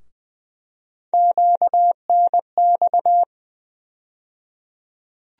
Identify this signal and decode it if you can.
Morse code, heard as a keyed tone, about 20 wpm, 700 Hz. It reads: QNX
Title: USA Call Sign Postfixes - 3 Letters 20wpm